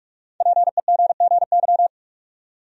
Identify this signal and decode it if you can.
Morse code, heard as a keyed tone, about 46 wpm, 700 Hz. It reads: PEGGY
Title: USA Common Names - Max 5 Letters - 45wpm